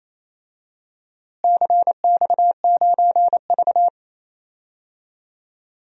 Morse code, heard as a keyed tone, about 28 wpm, 700 Hz. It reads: CX9V
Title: DXCC-100 Call Signs - 4 Characters - Repeated 3x - 28wpm